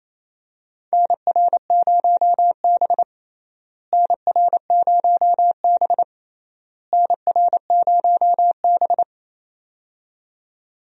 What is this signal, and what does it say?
Morse code, heard as a keyed tone, about 28 wpm, 700 Hz. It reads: NR0B NR0B NR0B
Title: USA Call Signs - 4 Characters - Repeated 3x - 28wpm